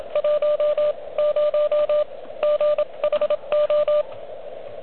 Morse code, heard as a keyed tone, about 25 wpm, 600 Hz, with radio noise. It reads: 1 0 GHO
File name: OK QRP 2022 / Zvuk / REC31